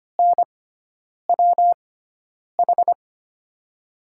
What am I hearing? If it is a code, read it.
Morse code, heard as a keyed tone, about 25 wpm, spaced as for 9 wpm, 700 Hz. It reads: NWH
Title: USA Call Signs - 6 Characters - 6x Character Spacing - 25wpm